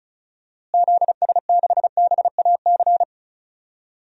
Morse code, heard as a keyed tone, about 36 wpm, 700 Hz. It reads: ZS6BAC
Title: DXCC-100 Call Signs - 6 Characters - Repeated 3x - 35wpm